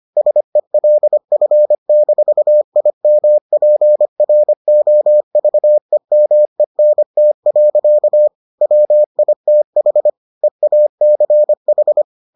Morse code, heard as a keyed tone, about 25 wpm, 600 Hz. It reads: SELF-IMPROVEMENT. WITH EACH